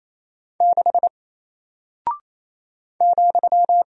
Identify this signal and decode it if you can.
Morse code, heard as a keyed tone, about 28 wpm, 700 Hz. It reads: 6 ,
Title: Single Character (Letter:Number:Punctuation:Prosign) Alt 28wpm